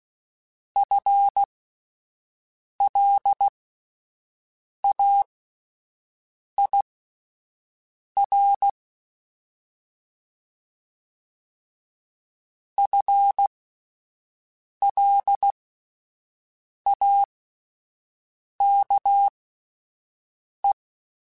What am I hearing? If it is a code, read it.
Morse code, heard as a keyed tone, about 16 wpm, spaced as for 5 wpm, 800 Hz. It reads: FLAIR FLAKE
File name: Alphabetical / Alphabetical mp3s / Fs @5